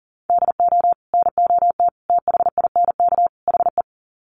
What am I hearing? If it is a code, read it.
Morse code, heard as a keyed tone, about 39 wpm, 700 Hz. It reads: DO NOT THINK HE
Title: Sets of 4 Words 40wpm